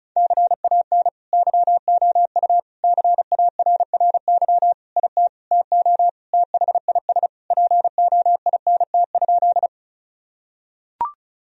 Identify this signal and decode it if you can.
Morse code, heard as a keyed tone, about 36 wpm, 700 Hz. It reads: CAN YOU CARRY IT TO THIS POINT?
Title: Sentences from Top 1000 Words - Review Part 1 - 35wpm